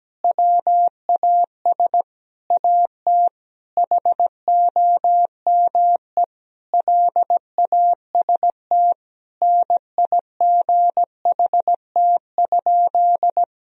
Morse code, heard as a keyed tone, about 17 wpm, 700 Hz. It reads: WAS AT HOME LAST NIGHT?
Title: Sentences from Top 1000 Words - Review Part 2 - 17wpm